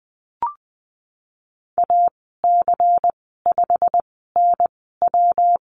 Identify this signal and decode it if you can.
Morse code, heard as a keyed tone, about 20 wpm, spaced as for 15 wpm, 700 Hz. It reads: AC5NW
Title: 128 Lesson 21g - Number 6 - Call Signs 20wpm